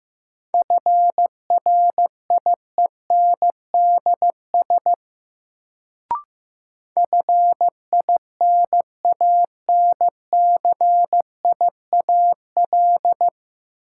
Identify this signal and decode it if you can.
Morse code, heard as a keyed tone, about 15 wpm, 700 Hz. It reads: FRIENDS FINANCIAL
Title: Sentences from Top 1000 Words - Review Part 5 - New Word Drill - 15wpm